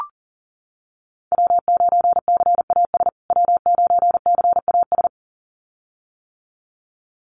Morse code, heard as a keyed tone, about 37 wpm, 700 Hz. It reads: W9CAS W9CAS
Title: USA Call Signs Repeated 2x 40wpm